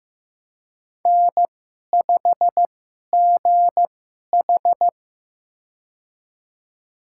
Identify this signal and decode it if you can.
Morse code, heard as a keyed tone, about 15 wpm, spaced as for 11 wpm, 700 Hz. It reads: N5GH
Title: USA Call Signs - 4 Characters - 2x Character Spacing - 15wpm